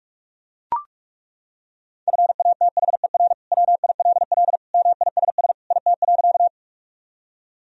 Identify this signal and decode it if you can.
Morse code, heard as a keyed tone, about 41 wpm, 700 Hz. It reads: FATHER WILL MISS IT.
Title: Sentences from Top 1000 Words - Review Part 3 - 45wpm